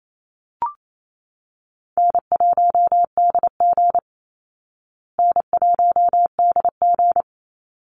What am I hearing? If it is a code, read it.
Morse code, heard as a keyed tone, about 28 wpm, 700 Hz. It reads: N1DG N1DG
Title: Easy Call Signs Repeated 2x - 28wpm